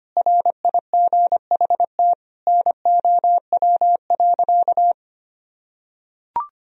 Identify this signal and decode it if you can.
Morse code, heard as a keyed tone, about 25 wpm, 700 Hz. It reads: RIGHT NOW.
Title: Sentences from Top 1000 Words - Review Part 4 - 25wpm